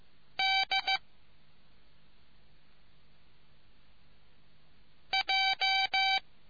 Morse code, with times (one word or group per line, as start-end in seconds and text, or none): 0.4-1.0 D
5.1-6.2 J